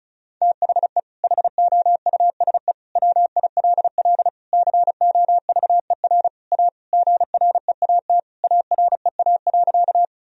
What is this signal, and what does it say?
Morse code, heard as a keyed tone, about 34 wpm, 700 Hz. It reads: THE HOUSE WILL COVER A GREAT AREA.
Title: Sentences from Top 1000 Words - Review Part 2 - 35wpm